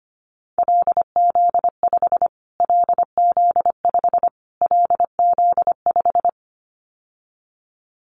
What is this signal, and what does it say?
Morse code, heard as a keyed tone, about 25 wpm, 700 Hz. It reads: LZ5 LZ5 LZ5
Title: Callsign Prefixes for Top 25 DXCC - Repeated 3x - 25wpm